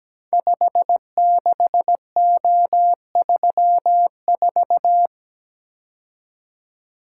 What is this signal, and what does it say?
Morse code, heard as a keyed tone, about 17 wpm, 700 Hz. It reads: 56O34
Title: Code Groups - 5 Letter:Numbers - 17wpm